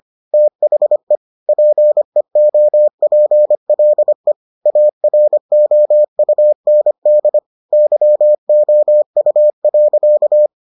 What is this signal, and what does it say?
Morse code, heard as a keyed tone, about 25 wpm, 600 Hz. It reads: THE PEOPLE AROUND YOU.